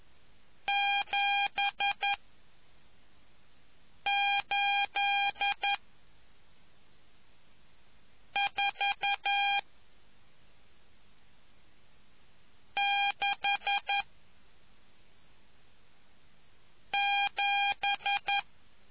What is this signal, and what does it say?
Morse code, heard as a keyed tone, about 11 wpm, 800 Hz. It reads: 7 8 4 6 7